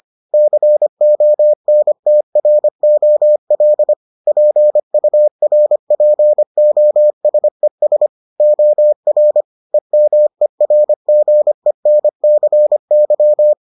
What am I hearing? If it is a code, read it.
Morse code, heard as a keyed tone, about 25 wpm, 600 Hz. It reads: CONTROL PURPOSES OR EMERGENCY